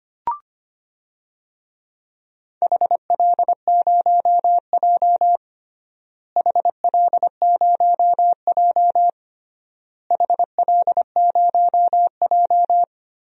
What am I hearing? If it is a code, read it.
Morse code, heard as a keyed tone, about 25 wpm, 700 Hz. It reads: HL0J HL0J HL0J
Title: DXCC-100 Call Signs - 4 Characters - Repeated 3x - 25wpm